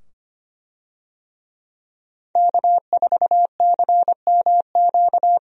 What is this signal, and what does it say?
Morse code, heard as a keyed tone, about 25 wpm, 700 Hz. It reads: K4CMQ
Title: USA Call Signs - 5 Characters - 25wpm